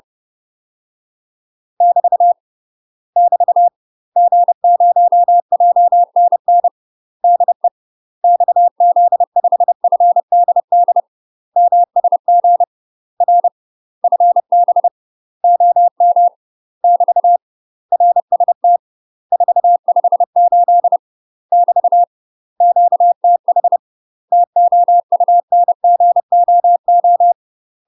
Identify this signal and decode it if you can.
Morse code, heard as a keyed tone, about 30 wpm, 700 Hz. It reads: <BT> <BT> G0JNN DE XZ5FDD MSG R FB OM <BT> RST 458 <BT> QTH TOUNGOO
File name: UK MASTER QSOs TRAINING / 30WPM / UK MASTER QSOs 0034